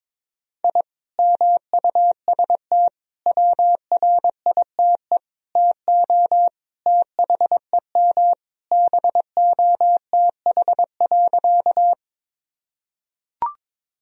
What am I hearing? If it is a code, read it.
Morse code, heard as a keyed tone, about 21 wpm, 700 Hz. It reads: I MUST WRITE TO THEM BOTH.
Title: Sentences from Top 1000 Words - Review Part 1 - 22wpm